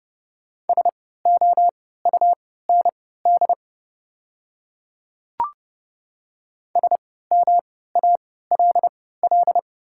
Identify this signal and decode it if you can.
Morse code, heard as a keyed tone, about 29 wpm, spaced as for 17 wpm, 690 Hz. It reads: SOUND SMALL
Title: 3x Character Spacing - Top 200 Words - 30wpm